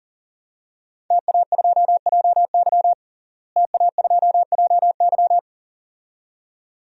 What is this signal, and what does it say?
Morse code, heard as a keyed tone, about 40 wpm, 700 Hz. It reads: TA2JY TA2JY
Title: DXCC-100 Call Signs - 5 Characters - Repeated 2x - 40wpm